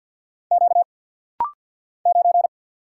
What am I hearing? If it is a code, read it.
Morse code, heard as a keyed tone, about 49 wpm, 700 Hz. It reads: Q 9
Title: Single Letter-Number Alt - Rapid-Fire 50wpm